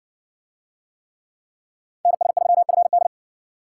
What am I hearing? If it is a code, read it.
Morse code, heard as a keyed tone, about 60 wpm, 700 Hz. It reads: NS3FD